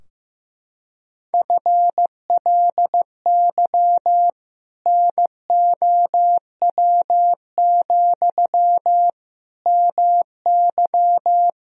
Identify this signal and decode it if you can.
Morse code, heard as a keyed tone, about 15 wpm, 700 Hz. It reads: FLY NOW, MY
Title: Sentences from Top 1000 Words - Review Part 2 - 15wpm